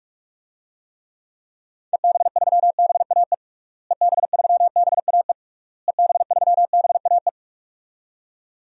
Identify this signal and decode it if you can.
Morse code, heard as a keyed tone, about 46 wpm, 700 Hz. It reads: EB3BAE EB3BAE EB3BAE
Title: DXCC-100 Call Signs - 6 Characters - Repeated 3x - 45wpm